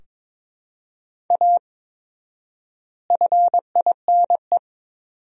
Morse code, heard as a keyed tone, about 21 wpm, 700 Hz. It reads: A FINE
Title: 4x Word-Spacing - Sentences from Top 500 Words - 22wpm